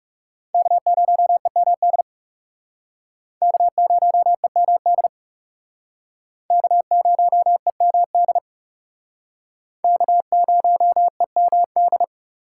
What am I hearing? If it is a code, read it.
Morse code, heard as a keyed tone, about 40 wpm, 700 Hz. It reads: K0EMD K0EMD K0EMD K0EMD